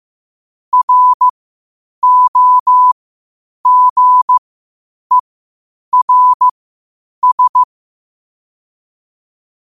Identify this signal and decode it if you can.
Morse code, heard as a keyed tone, about 15 wpm, spaced as for 8 wpm, 1000 Hz. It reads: ROGERS
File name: qotdfiles / 8wpm